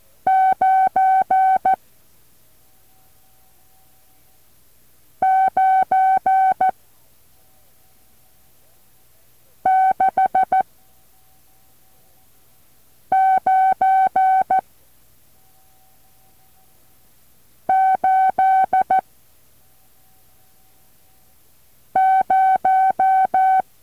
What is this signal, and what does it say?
Morse code, heard as a keyed tone, about 14 wpm, 760 Hz. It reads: 9 9 6 9 8 0
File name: intl morse code2